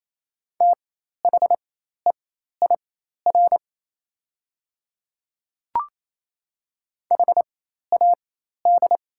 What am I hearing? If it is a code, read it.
Morse code, heard as a keyed tone, about 28 wpm, spaced as for 13 wpm, 700 Hz. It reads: THEIR HAD
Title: 4x Character Spacing - Top 100 Words - 28wpm